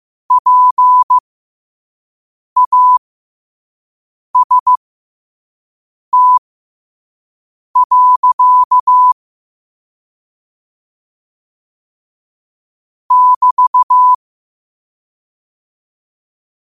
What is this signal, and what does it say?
Morse code, heard as a keyed tone, about 15 wpm, spaced as for 5 wpm, 1000 Hz. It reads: PAST. <BT>
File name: qotdfiles / 5wpm